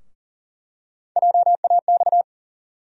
Morse code, heard as a keyed tone, about 40 wpm, 700 Hz. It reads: JAX
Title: USA Call Sign Postfixes - 3 Letters 40wpm